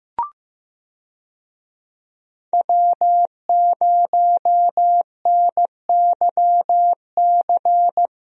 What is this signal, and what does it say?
Morse code, heard as a keyed tone, about 16 wpm, 700 Hz. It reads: W0NYC
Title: USA Call Signs - 5 Characters - Repeated 2x - 15wpm